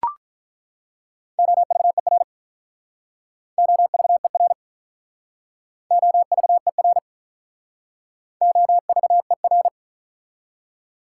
Morse code, heard as a keyed tone, about 48 wpm, 700 Hz. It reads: OVER OVER OVER OVER